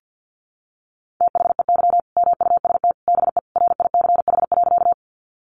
Morse code, heard as a keyed tone, about 50 wpm, 700 Hz. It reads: THEY MUST BE RICH.
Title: Sentences from Top 1100 Words 50wpm